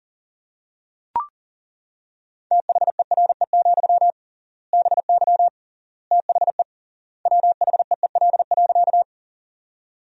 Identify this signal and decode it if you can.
Morse code, heard as a keyed tone, about 37 wpm, 700 Hz. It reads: THERE, BY THE WHEEL.
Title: 3x Word-Spacing - Sentences from Top 500 Words - 40wpm